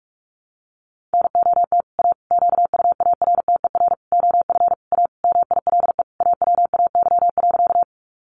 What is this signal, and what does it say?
Morse code, heard as a keyed tone, about 44 wpm, 700 Hz. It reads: NOT A QUARTER OF A MILE AWAY.
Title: Sentences from Top 1300 Words 45wpm